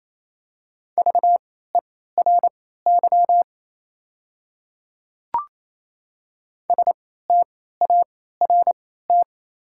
Morse code, heard as a keyed tone, about 27 wpm, spaced as for 16 wpm, 690 Hz. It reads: VERY START
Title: 3x Character Spacing - Top 300 Words - 28wpm